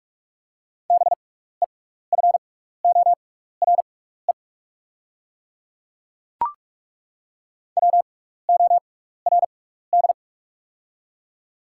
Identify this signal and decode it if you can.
Morse code, heard as a keyed tone, about 42 wpm, spaced as for 15 wpm, 690 Hz. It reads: BEFORE WORD
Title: 6x Character Spacing - Top 200 Words - 45wpm